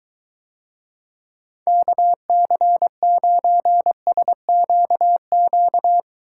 Morse code, heard as a keyed tone, about 22 wpm, 700 Hz. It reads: KC9SQQ